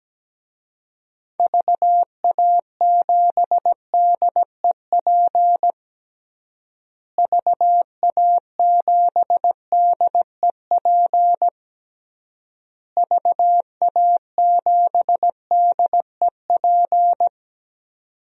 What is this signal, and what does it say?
Morse code, heard as a keyed tone, about 17 wpm, 700 Hz. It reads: VA7DEP VA7DEP VA7DEP
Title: DXCC-100 Call Signs - 6 Characters - Repeated 3x - 17wpm